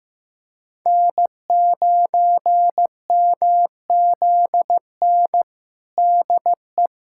Morse code, heard as a keyed tone, about 15 wpm, 700 Hz. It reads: N9MZN DE
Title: USA Callsign DE Callsign - 5 Characters - 15wpm